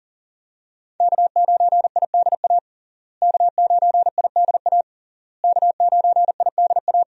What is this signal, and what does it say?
Morse code, heard as a keyed tone, about 40 wpm, 700 Hz. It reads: K9IDA K9IDA K9IDA
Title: USA Call Signs - 5 Characters - Repeated 3x - 40wpm